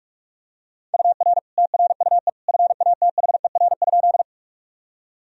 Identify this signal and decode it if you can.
Morse code, heard as a keyed tone, about 46 wpm, 700 Hz. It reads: UR TRUE FATHER?
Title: Sentences from Top 400 Words and Abbreviations 45wpm